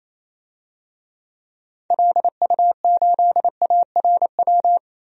Morse code, heard as a keyed tone, about 28 wpm, 700 Hz. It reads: LU8ARW
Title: DXCC-100 Call Signs - 6 Characters - 28wpm